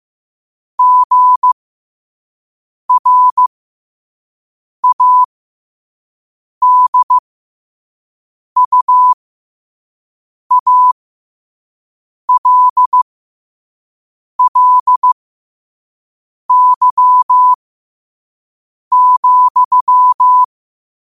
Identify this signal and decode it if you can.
Morse code, heard as a keyed tone, about 15 wpm, spaced as for 5 wpm, 1000 Hz. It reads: GRADUALLY,
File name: qotdfiles / 5wpm